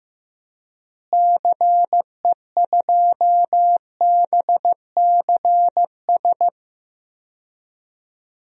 Morse code, heard as a keyed tone, about 15 wpm, 700 Hz. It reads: CE2BCS
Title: DXCC-100 Call Signs - 6 Characters - Repeated 2x - 15wpm